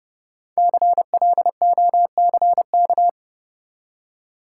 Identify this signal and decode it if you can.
Morse code, heard as a keyed tone, about 30 wpm, 700 Hz. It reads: CLOCK
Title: Five Letter Words - 30wpm